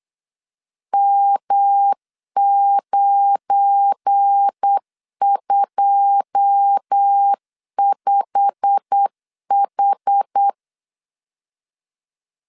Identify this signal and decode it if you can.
Morse code, heard as a keyed tone, about 8 wpm, 800 Hz. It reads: M925H
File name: LEZIONI / 9 2 8wpm 3437